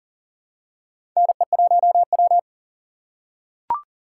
Morse code, heard as a keyed tone, about 40 wpm, 700 Hz. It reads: NE1W
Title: USA Call Signs - 4 Characters - 3x Character Spacing - 40wpm